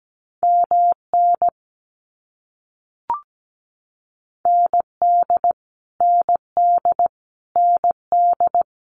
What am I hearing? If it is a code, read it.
Morse code, heard as a keyed tone, about 17 wpm, 700 Hz. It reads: MN ND ND ND
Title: US States Repeated 3x 17wpm